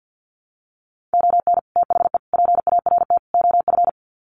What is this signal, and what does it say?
Morse code, heard as a keyed tone, about 50 wpm, 700 Hz. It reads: ON THE PART OF
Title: Sets of 4 Words 50wpm